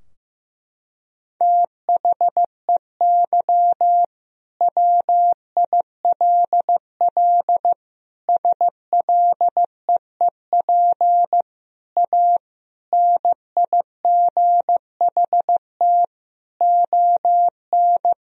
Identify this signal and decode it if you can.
Morse code, heard as a keyed tone, about 15 wpm, 700 Hz. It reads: THEY WILL SLEEP A NIGHT ON
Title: Sentences from Top 1000 Words - Review Part 2 - 15wpm